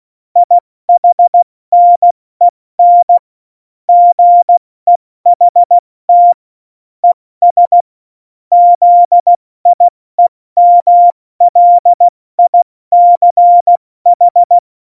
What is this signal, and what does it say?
Morse code, heard as a keyed tone, about 16 wpm, 700 Hz. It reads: IHNEN GEHT ES ZIEMLICH
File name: Morsekurs / Witze / Witz 20. Februar 2018, 70 F80